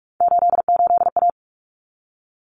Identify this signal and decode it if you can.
Morse code, heard as a keyed tone, about 44 wpm, 700 Hz. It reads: 88A
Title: Common Cut Numbers - 3-Digit 45wpm